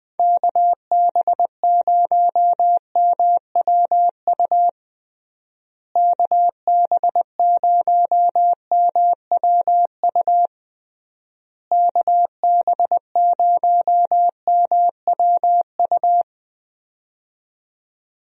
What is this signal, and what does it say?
Morse code, heard as a keyed tone, about 20 wpm, 700 Hz. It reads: KB0MWU KB0MWU KB0MWU
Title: USA Call Signs - 6 Characters - Repeated 3x - 20wpm